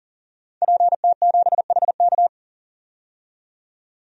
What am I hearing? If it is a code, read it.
Morse code, heard as a keyed tone, about 40 wpm, 700 Hz. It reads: PT7HK
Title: DXCC-100 Call Signs - 5 Characters - 40wpm